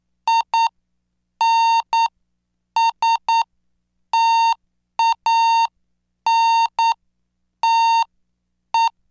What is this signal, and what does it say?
Morse code, heard as a keyed tone, about 9 wpm, spaced as for 7 wpm, 910 Hz. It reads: INSTANTE